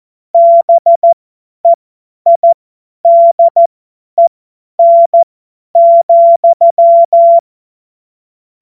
Morse code, heard as a keyed tone, about 14 wpm, spaced as for 10 wpm, 680 Hz. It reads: BEIDEN,
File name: Morsekurs / Witze / Witz 05 Dezember 2023 50 F70